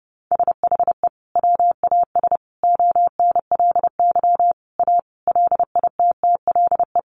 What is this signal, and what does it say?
Morse code, heard as a keyed tone, about 30 wpm, 700 Hz. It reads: SHE WAS ONLY A LITTLE
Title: Sentences from Top 400 Words 30wpm